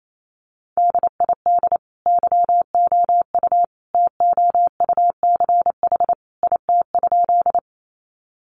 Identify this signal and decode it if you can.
Morse code, heard as a keyed tone, about 28 wpm, 700 Hz. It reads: DID YOU TOUCH IT?
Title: Sentences from Top 1200 Words 28wpm